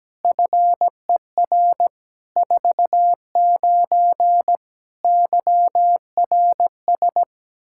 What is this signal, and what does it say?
Morse code, heard as a keyed tone, about 17 wpm, 700 Hz. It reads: FER 49 YRS
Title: K1USN SST QSO Practice 17wpm